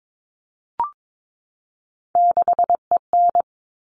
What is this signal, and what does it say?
Morse code, heard as a keyed tone, about 22 wpm, 700 Hz. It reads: 6EN
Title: All Cut Numbers - 3-Digit 22wpm